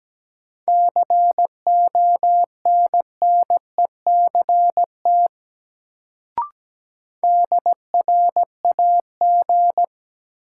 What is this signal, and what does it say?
Morse code, heard as a keyed tone, about 17 wpm, 700 Hz. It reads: CONNECT DRAG
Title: Sentences from Top 1500 Words - New Word Drill - 17wpm